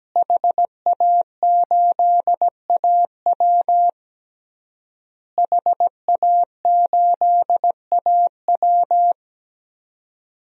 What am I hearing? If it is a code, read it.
Morse code, heard as a keyed tone, about 17 wpm, 700 Hz. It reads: HA8AW HA8AW
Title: DXCC-100 Call Signs - 5 Characters - Repeated 3x - 17wpm